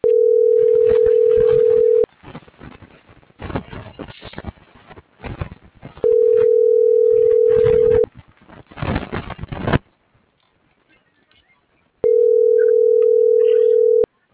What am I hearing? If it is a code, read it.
Ringback tone (North America), 440 + 480 Hz.